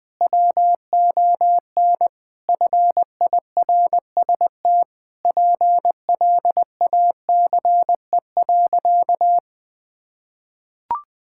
Morse code, heard as a keyed tone, about 21 wpm, 700 Hz. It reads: WON FIRST PLACE.
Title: Sentences with 7-Word Phrases - Spoken Beginning - 20wpm